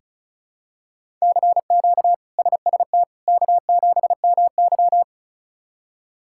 Morse code, heard as a keyed tone, about 34 wpm, 700 Hz. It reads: CQ SST K7MY
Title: K1USN SST QSO Practice - 35wpm